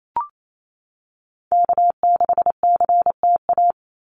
Morse code, heard as a keyed tone, about 28 wpm, 700 Hz. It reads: K6CTA
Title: Call Signs Repeated 2x - 28wpm